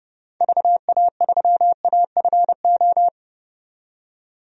Morse code, heard as a keyed tone, about 32 wpm, 690 Hz. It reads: VA3AFO
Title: DXCC-100 Call Signs - 6 Characters - 30wpm